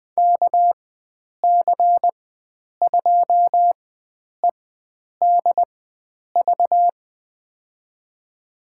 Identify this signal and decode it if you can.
Morse code, heard as a keyed tone, about 20 wpm, spaced as for 9 wpm, 700 Hz. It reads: KC2EDV
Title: USA Call Signs - 6 Characters - 4x Character Spacing - 20wpm